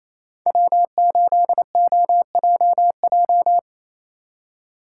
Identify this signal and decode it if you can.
Morse code, heard as a keyed tone, about 36 wpm, 700 Hz. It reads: W8OJJ